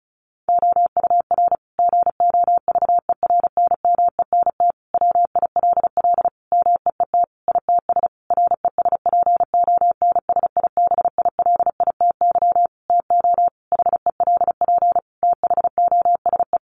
Morse code, heard as a keyed tone, about 34 wpm, 700 Hz. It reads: OUR GOVERNMENT WILL MEET ITS RESPONSIBILITY TO HELP THOSE
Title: Sentences from Top 1200 Words 35wpm